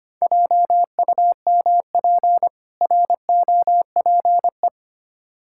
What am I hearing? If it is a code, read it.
Morse code, heard as a keyed tone, about 25 wpm, 700 Hz. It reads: JUMP ROPE
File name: practice / 25wpm / Difficult Words with Hints - 25wpm